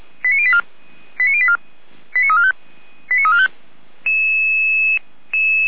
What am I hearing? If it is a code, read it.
Five-tone selective call (ZVEI), heard as quick runs of five tones, tones of about 80 ms. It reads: 89084 89084 89345 89345